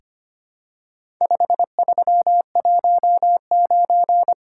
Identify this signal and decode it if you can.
Morse code, heard as a keyed tone, about 25 wpm, 700 Hz. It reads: 5319
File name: practice / 25wpm / Four Letter-Number Combinations - 25wpm